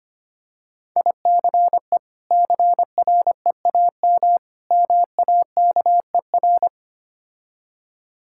Morse code, heard as a keyed tone, about 25 wpm, 700 Hz. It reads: ICE CREAM MAKER
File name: practice / 25wpm / Difficult Words - Speed-Racing with Hints - 25wpm